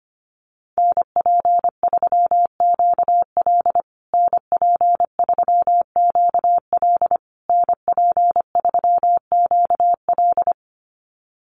Morse code, heard as a keyed tone, about 25 wpm, 700 Hz. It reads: NP3QL NP3QL NP3QL
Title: USA Call Signs Repeated 3x 25wpm